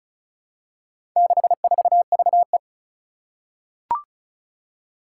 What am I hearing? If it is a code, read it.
Morse code, heard as a keyed tone, about 36 wpm, 700 Hz. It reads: 64VE
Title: Four Challenging Letter-Number Combinations 35wpm